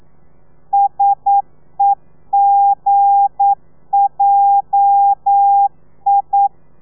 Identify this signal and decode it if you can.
Morse code, heard as a keyed tone, about 9 wpm, 800 Hz. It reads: SEGJI
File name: telegrafia / i2tfj / Lez8 40